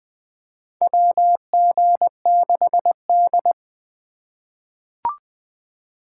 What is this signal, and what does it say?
Morse code, heard as a keyed tone, about 20 wpm, 700 Hz. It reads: WG6D
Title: USA Call Signs - 2x Character Spacing - 20wpm